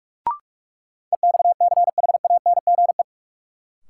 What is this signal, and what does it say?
Morse code, heard as a keyed tone, about 41 wpm, 700 Hz. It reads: EXCHANGE
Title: Sets of 5 Words - New Word Drill - 45wpm